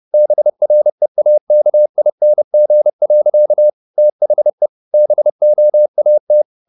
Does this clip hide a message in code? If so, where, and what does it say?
Yes: Morse code from 0.1 to 6.4 s, heard as a keyed tone, about 30 wpm, 600 Hz. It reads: BREAKING. THE BOAT